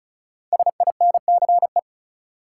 Morse code, heard as a keyed tone, about 34 wpm, 700 Hz. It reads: SINCE
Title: Five Letter Words - 35wpm